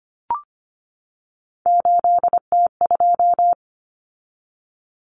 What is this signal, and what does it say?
Morse code, heard as a keyed tone, about 25 wpm, 700 Hz. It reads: 8T2
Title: Common Cut Numbers - 3-Digit 25wpm